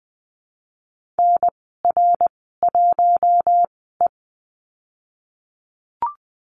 Morse code, heard as a keyed tone, about 19 wpm, spaced as for 14 wpm, 700 Hz. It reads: NR1E
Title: 055 Lesson 10g - Letter r - Call Signs 20wpm